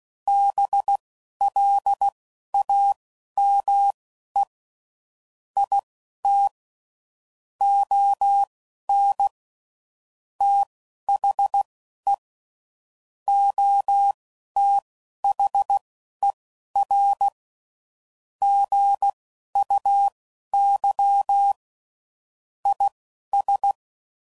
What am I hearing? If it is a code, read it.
Morse code, heard as a keyed tone, about 16 wpm, spaced as for 11 wpm, 800 Hz. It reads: BLAME IT ON THE OTHER GUY IS